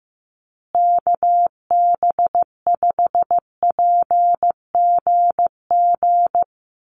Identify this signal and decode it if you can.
Morse code, heard as a keyed tone, about 15 wpm, 700 Hz. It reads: KB5PGG
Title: USA Call Signs Repeated 2x 15wpm